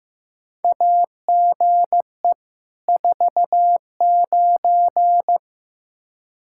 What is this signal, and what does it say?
Morse code, heard as a keyed tone, about 15 wpm, 700 Hz. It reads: AGE 49
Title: Practice QSO - Exchange 3 - More Info - 15wpm